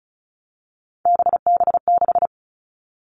Morse code, heard as a keyed tone, about 34 wpm, 700 Hz. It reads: BB6
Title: All Cut Numbers - 3-Digit 35wpm